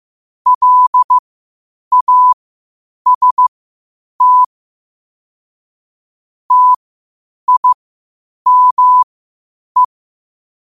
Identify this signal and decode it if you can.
Morse code, heard as a keyed tone, about 15 wpm, spaced as for 8 wpm, 1000 Hz. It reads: LAST TIME
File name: qotdfiles / 8wpm